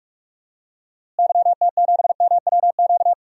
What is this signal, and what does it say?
Morse code, heard as a keyed tone, about 44 wpm, 700 Hz. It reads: YT7MWQ
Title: DXCC-100 Call Signs - 6 Characters - Repeated 3x - 45wpm